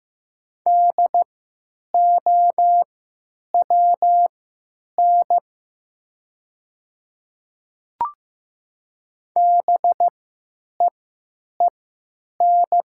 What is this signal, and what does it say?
Morse code, heard as a keyed tone, about 15 wpm, spaced as for 9 wpm, 700 Hz. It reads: DOWN BEEN
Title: 3x Character Spacing - Top 100 Words - 15wpm